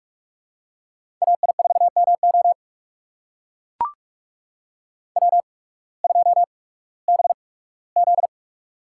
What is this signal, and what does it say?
Morse code, heard as a keyed tone, about 44 wpm, spaced as for 21 wpm, 700 Hz. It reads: AI4MO W2BZ
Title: USA Call Signs - 8x Character Spacing - 45wpm